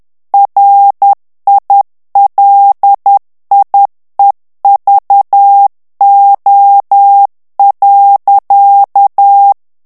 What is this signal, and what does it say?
Morse code, heard as a keyed tone, about 10 wpm, 800 Hz. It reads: RILIEVO.
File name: ITALIEN texte clair 2 vitesse 10 mots minute